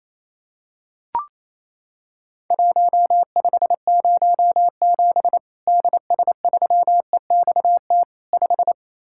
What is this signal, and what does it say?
Morse code, heard as a keyed tone, about 28 wpm, 700 Hz. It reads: 1507 DS3EXT 5
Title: Worked All Europe DX Contest - QTC Exchange - 28wpm